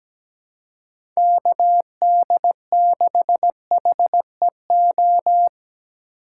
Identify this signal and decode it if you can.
Morse code, heard as a keyed tone, about 17 wpm, 700 Hz. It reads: KD6HEO
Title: USA Call Signs - 4x Character Spacing - 17wpm